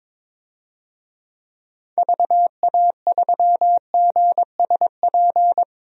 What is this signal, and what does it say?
Morse code, heard as a keyed tone, about 22 wpm, 700 Hz. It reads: VA3GSP
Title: DXCC-100 Call Signs - 6 Characters - 22wpm